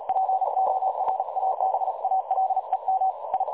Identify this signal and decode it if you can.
Morse code, heard as a keyed tone, about 38 wpm, 760 Hz, with noise. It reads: PORABALEM I
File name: dlazaawansowanych / nagraniazpasma / websdr recording 3565.0kHz